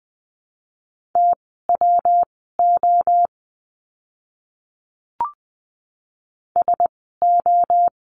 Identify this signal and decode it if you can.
Morse code, heard as a keyed tone, about 20 wpm, spaced as for 15 wpm, 700 Hz. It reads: TWO SO
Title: 230 Lesson 37f - Letter x - Words 20wpm